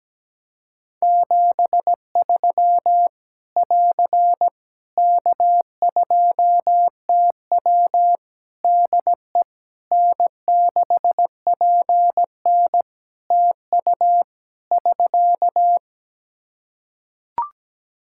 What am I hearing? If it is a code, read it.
Morse code, heard as a keyed tone, about 17 wpm, 700 Hz. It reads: 73 <AR> K2TW DE N6PN TU <SK>
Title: Practice QSO - Exchange 4 - Ending - 17wpm